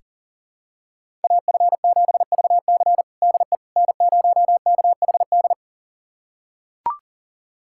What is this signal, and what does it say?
Morse code, heard as a keyed tone, about 40 wpm, 700 Hz. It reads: AF7VC DE N0KHD
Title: USA Callsign DE Callsign - 5 Characters - 40wpm